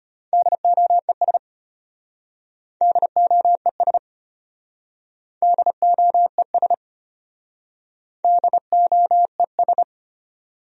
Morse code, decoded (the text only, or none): DOES DOES DOES DOES